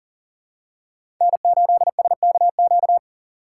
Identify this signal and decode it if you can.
Morse code, heard as a keyed tone, about 39 wpm, 690 Hz. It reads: N8SKQ
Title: USA Call Signs - 3x Character Spacing - 40wpm